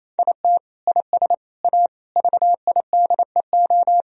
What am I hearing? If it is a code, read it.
Morse code, heard as a keyed tone, about 28 wpm, 700 Hz. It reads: IT IS A VIDEO
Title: Sentences from Top 1000 Words - Review Part 4 - 28wpm